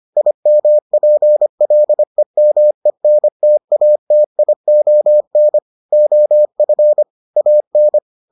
Morse code, heard as a keyed tone, about 25 wpm, 600 Hz. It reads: IMPLEMENTATION OF AN